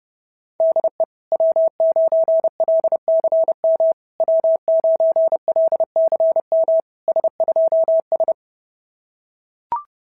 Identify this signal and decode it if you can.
Morse code, heard as a keyed tone, about 29 wpm, 690 Hz. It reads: DE W9LCM W9LCM S2S
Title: SOTA QSO Practice 30wpm